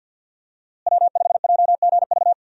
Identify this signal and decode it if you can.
Morse code, heard as a keyed tone, about 50 wpm, 700 Hz. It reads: W5JGV
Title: USA Call Signs - 5 Characters - Repeated 2x - 50wpm